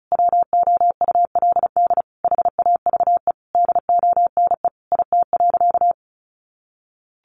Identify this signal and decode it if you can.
Morse code, heard as a keyed tone, about 34 wpm, 700 Hz. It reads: WOULD HAVE DONE IT.
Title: Sentences from Top 1200 Words 35wpm